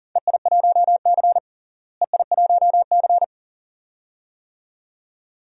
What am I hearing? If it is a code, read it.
Morse code, heard as a keyed tone, about 40 wpm, 700 Hz. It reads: EI1C EI1C
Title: DXCC-100 Call Signs - 4 Characters - Repeated 2x - 40wpm